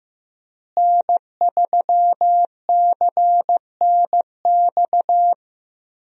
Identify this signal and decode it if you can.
Morse code, heard as a keyed tone, about 15 wpm, 700 Hz. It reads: N3CNX
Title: USA Call Signs - 5 Characters - Repeated 2x - 15wpm